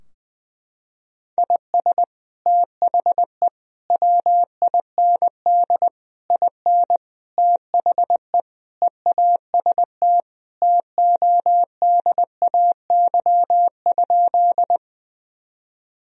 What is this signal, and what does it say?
Morse code, heard as a keyed tone, about 21 wpm, 700 Hz. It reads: IS THE WIND IN THE EAST TODAY?
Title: Sentences from Top 1000 Words - Review Part 3 - 20wpm